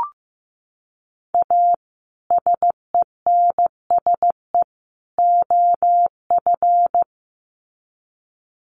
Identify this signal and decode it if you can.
Morse code, heard as a keyed tone, about 15 wpm, 700 Hz. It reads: A SENSE OF
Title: Sets of 3 Words 15wpm